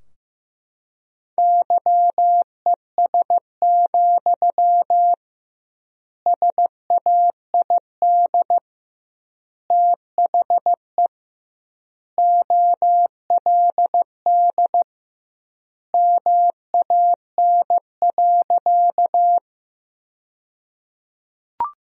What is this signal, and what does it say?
Morse code, heard as a keyed tone, about 15 wpm, 700 Hz. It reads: YES, SAID THE OLD MAN.
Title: 2x Word-Spacing - Sentences from Top 500 Words - 15wpm